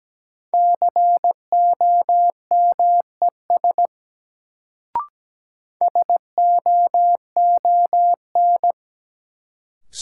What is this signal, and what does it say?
Morse code, heard as a keyed tone, about 17 wpm, 700 Hz. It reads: COMES SOON
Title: Sets of 3 Words - New Word Drill - 17wpm